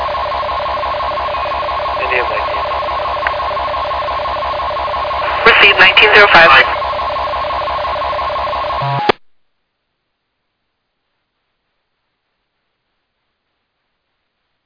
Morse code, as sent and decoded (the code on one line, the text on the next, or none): none